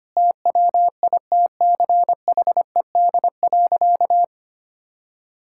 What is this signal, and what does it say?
Morse code, heard as a keyed tone, about 25 wpm, 700 Hz. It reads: TWITCHED.
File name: practice / 25wpm / Sentences with 2-Word Phrases - 25wpm